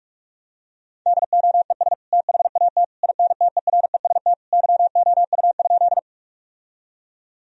Morse code, heard as a keyed tone, about 46 wpm, 700 Hz. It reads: DOES THAT INTEREST YOU?
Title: Sentences from Top 1000 Words - Review Part 2 - 45wpm